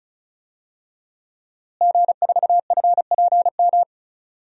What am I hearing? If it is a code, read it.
Morse code, heard as a keyed tone, about 36 wpm, 700 Hz. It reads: G4FPM
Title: DXCC-100 Call Signs - 5 Characters - Repeated 2x - 35wpm